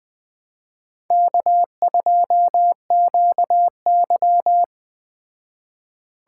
Morse code, heard as a keyed tone, about 19 wpm, 700 Hz. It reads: K2QY